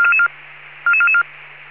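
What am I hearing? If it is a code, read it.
Five-tone selective call (ZVEI), heard as quick runs of five tones, tones of about 80 ms. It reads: 44444
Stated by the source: ZVEI five-tone selective call received by SDR.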